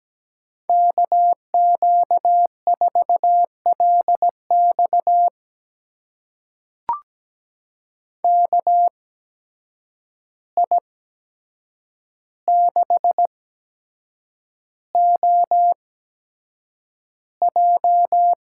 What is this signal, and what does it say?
Morse code, heard as a keyed tone, about 17 wpm, spaced as for 7 wpm, 700 Hz. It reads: KQ4LX KI6OJ
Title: USA Call Signs - 5 Characters - 8x Character Spacing - 17wpm